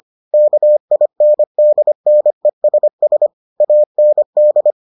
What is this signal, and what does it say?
Morse code, heard as a keyed tone, about 25 wpm, 600 Hz. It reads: KINDNESS AND